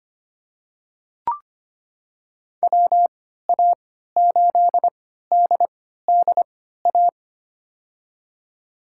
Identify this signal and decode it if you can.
Morse code, heard as a keyed tone, about 25 wpm, spaced as for 14 wpm, 700 Hz. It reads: WA8DDA
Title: USA Call Signs - 6 Characters - 3x Character Spacing - 25wpm